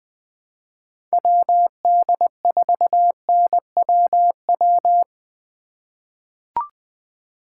Morse code, heard as a keyed tone, about 21 wpm, 700 Hz. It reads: WD4NWW
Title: USA Call Signs - 6 Characters - 2x Character Spacing - 20wpm